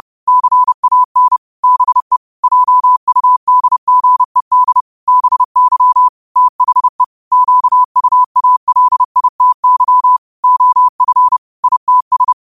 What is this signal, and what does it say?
Morse code, heard as a keyed tone, about 30 wpm, 1000 Hz. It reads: CAN BE JUDGED BY THE QUALITY OF ITS
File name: qotdfiles / 30wpm